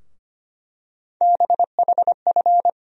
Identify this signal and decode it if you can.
Morse code, heard as a keyed tone, about 25 wpm, 700 Hz. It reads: BHF
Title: USA Call Sign Postfixes - 3 Letters 25wpm